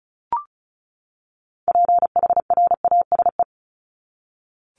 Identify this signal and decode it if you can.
Morse code, heard as a keyed tone, about 34 wpm, 700 Hz. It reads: PHRASE
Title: Top 1501-1750 Words 35wpm